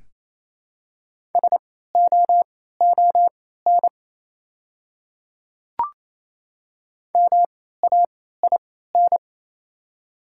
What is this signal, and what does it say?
Morse code, heard as a keyed tone, about 28 wpm, spaced as for 16 wpm, 690 Hz. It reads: SOON MAIN
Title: 3x Character Spacing - Top 500 Words - 28wpm